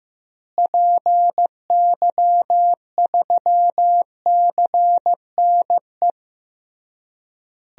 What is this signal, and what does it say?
Morse code, heard as a keyed tone, about 16 wpm, 700 Hz. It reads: PY3CNE
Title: DXCC-100 Call Signs - 6 Characters - Repeated 3x - 15wpm